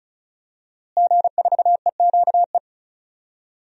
Morse code, heard as a keyed tone, about 34 wpm, 700 Hz. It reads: G4EQE
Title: DXCC-100 Call Signs - 5 Characters - Repeated 2x - 35wpm